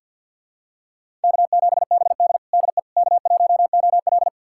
Morse code, heard as a keyed tone, about 47 wpm, 700 Hz. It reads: K7BD DE K1OL
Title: USA Callsign DE Callsign - 4 Characters - 50wpm